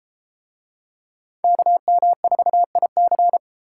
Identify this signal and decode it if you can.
Morse code, heard as a keyed tone, about 31 wpm, 690 Hz. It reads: KM4IC